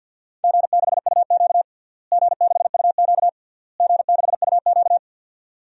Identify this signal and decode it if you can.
Morse code, heard as a keyed tone, about 50 wpm, 700 Hz. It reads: G6UQ G6UQ G6UQ
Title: DXCC-100 Call Signs - 4 Characters - Repeated 3x - 50wpm